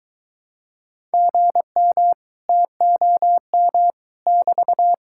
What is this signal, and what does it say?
Morse code, heard as a keyed tone, about 23 wpm, 700 Hz. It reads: GM TOM <BT>